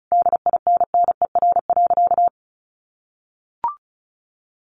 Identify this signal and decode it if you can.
Morse code, heard as a keyed tone, about 34 wpm, 700 Hz. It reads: DINNER.
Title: Easy Phrases 35wpm